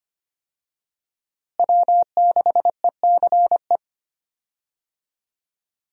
Morse code, heard as a keyed tone, about 25 wpm, 700 Hz. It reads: W6ECE
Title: USA Call Signs - 5 Characters - Repeated 2x - 25wpm